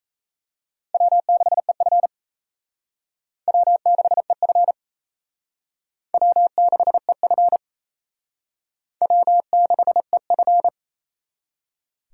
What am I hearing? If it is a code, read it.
Morse code, heard as a keyed tone, about 35 wpm, 700 Hz. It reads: W6EF W6EF W6EF W6EF